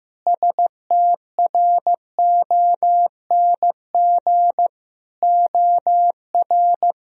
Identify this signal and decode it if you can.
Morse code, heard as a keyed tone, about 16 wpm, 700 Hz. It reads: STRONG OR
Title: Sentences from Top 1000 Words - Review Part 4 - 15wpm